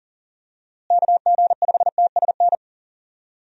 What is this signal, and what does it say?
Morse code, heard as a keyed tone, about 40 wpm, 700 Hz. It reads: KG5TSN
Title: USA Call Signs - 6 Characters - Repeated 2x - 40wpm